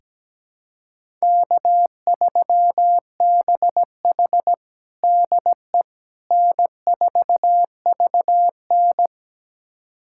Morse code, heard as a keyed tone, about 17 wpm, 700 Hz. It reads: K3BH DE N4VN
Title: USA Callsign DE Callsign - 4 Characters - 17wpm